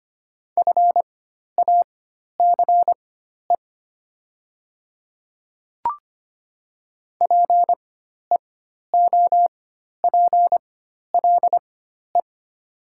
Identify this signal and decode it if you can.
Morse code, heard as a keyed tone, about 25 wpm, spaced as for 12 wpm, 700 Hz. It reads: FACE PEOPLE
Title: 4x Character Spacing - Top 300 Words - 25wpm